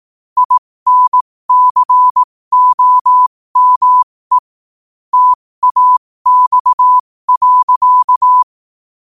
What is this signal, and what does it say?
Morse code, heard as a keyed tone, about 18 wpm, spaced as for 15 wpm, 1000 Hz. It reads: INCOME TAX.
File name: qotdfiles / 15wpm